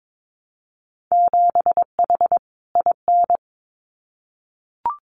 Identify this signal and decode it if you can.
Morse code, heard as a keyed tone, about 22 wpm, 700 Hz. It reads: 7H IN
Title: Winter Field Day Exchanges Encore 22wpm